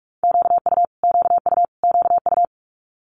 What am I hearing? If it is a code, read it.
Morse code, heard as a keyed tone, about 46 wpm, 700 Hz. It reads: QU QU QU
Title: Multi-letter Phonemes Repeated 3x 45wpm